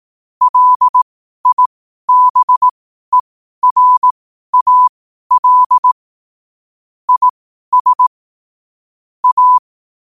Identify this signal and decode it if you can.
Morse code, heard as a keyed tone, about 18 wpm, spaced as for 12 wpm, 1000 Hz. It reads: LIBERAL IS A
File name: qotdfiles / 12wpm